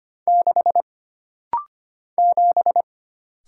Morse code, heard as a keyed tone, about 25 wpm, 700 Hz. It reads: 6 7
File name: Single Character (Letter:Number:Punctuation:Prosign) - Rapid-Fire - 25wpm